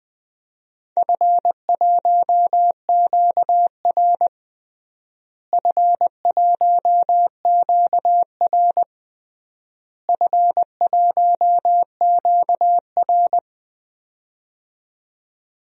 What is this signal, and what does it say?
Morse code, heard as a keyed tone, about 20 wpm, 700 Hz. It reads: F1QR F1QR F1QR
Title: DXCC-100 Call Signs - 4 Characters - Repeated 3x - 20wpm